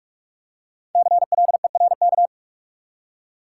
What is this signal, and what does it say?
Morse code, heard as a keyed tone, about 46 wpm, 700 Hz. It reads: CLERK
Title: Five Letter Words - 45wpm